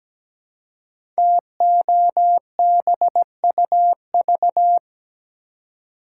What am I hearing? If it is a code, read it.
Morse code, heard as a keyed tone, about 17 wpm, 700 Hz. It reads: TOBUV
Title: Code Groups - 5 Letters - 17wpm